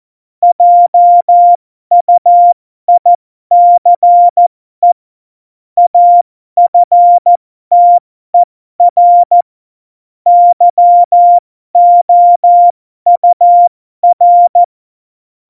Morse code, heard as a keyed tone, about 14 wpm, spaced as for 12 wpm, 700 Hz. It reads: JUICE AFTER YOUR
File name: Morsekurs / Witze